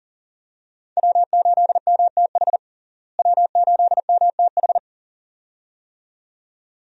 Morse code, heard as a keyed tone, about 40 wpm, 700 Hz. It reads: W8MTH W8MTH
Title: USA Call Signs - 5 Characters - Repeated 2x - 40wpm